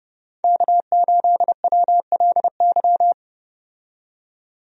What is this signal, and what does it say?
Morse code, heard as a keyed tone, about 31 wpm, 700 Hz. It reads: K8WLY
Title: USA Call Signs - 5 Characters - 30wpm